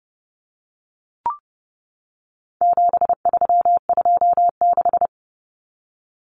Morse code, heard as a keyed tone, about 30 wpm, 700 Hz. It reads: 7326
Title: Common Cut Numbers - 4-Digit 30wpm